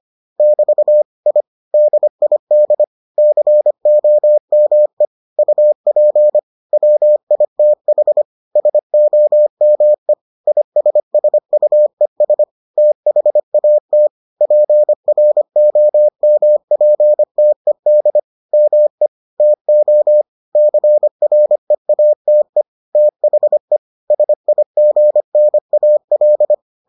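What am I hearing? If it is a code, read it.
Morse code, heard as a keyed tone, about 25 wpm, 600 Hz. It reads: <BT> I DID COME UP WITH SOME ISSUES THAT PROMPTED ME TO CREATE THE SIGNAL
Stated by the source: Morse edition of the Foundations of Amateur Radio episodes.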